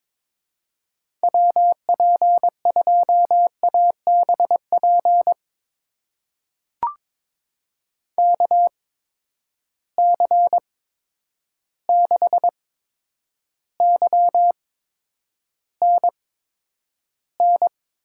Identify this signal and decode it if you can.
Morse code, heard as a keyed tone, about 22 wpm, spaced as for 9 wpm, 700 Hz. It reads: WP2ABP KC6YNN
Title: USA Call Signs - 6 Characters - 8x Character Spacing - 22wpm